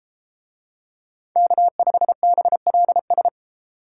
Morse code, heard as a keyed tone, about 33 wpm, 700 Hz. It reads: K5BLS